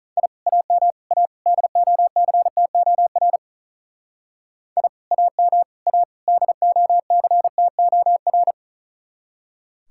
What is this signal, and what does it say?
Morse code, heard as a keyed tone, about 39 wpm, 700 Hz. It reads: I AM A DOCTOR I AM A DOCTOR
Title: Speed-Racing Common QSO Phrases Encore - 35wpm